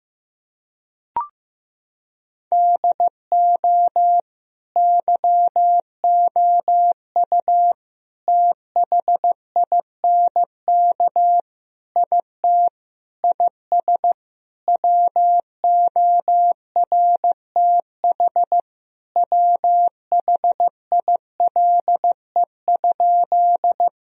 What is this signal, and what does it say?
Morse code, heard as a keyed tone, about 15 wpm, 700 Hz. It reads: DO YOU THINK IT IS WORTH WHILE?
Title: Sentences from Top 1000 Words - Review Part 1 - 15wpm